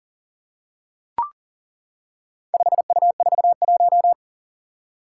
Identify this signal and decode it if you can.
Morse code, heard as a keyed tone, about 40 wpm, 700 Hz. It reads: 5U41
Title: Four Challenging Letter-Number Combinations 40wpm